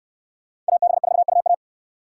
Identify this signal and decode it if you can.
Morse code, heard as a keyed tone, about 65 wpm, 700 Hz. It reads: AB4UA